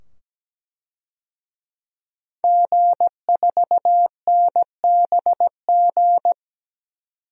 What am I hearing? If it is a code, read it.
Morse code, heard as a keyed tone, about 17 wpm, 700 Hz. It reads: G4NBG
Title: DXCC-100 Call Signs - 5 Characters - 17wpm